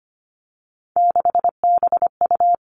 Morse code, heard as a keyed tone, about 25 wpm, 700 Hz. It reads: 6BU
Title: Three Challenging Letter-Number Combinations 25wpm